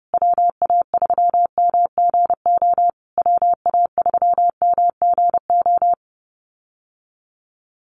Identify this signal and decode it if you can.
Morse code, heard as a keyed tone, about 30 wpm, 700 Hz. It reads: WA3MGO WA3MGO
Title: USA Call Signs Repeated 3x 30wpm